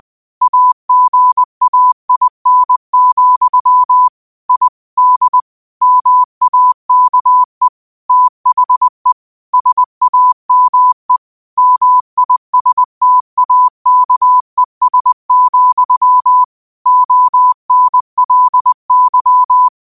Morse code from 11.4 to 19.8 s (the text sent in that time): MISTAKES, ONLY